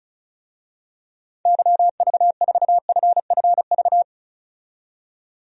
Morse code, heard as a keyed tone, about 36 wpm, 700 Hz. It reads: YV4FFV
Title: DXCC-100 Call Signs - 6 Characters - 35wpm